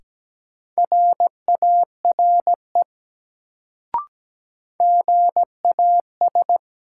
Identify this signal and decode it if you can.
Morse code, heard as a keyed tone, about 17 wpm, 700 Hz. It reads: RARE GAS
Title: Sentences from Top 1100 Words - New Word Drill Alt - 17wpm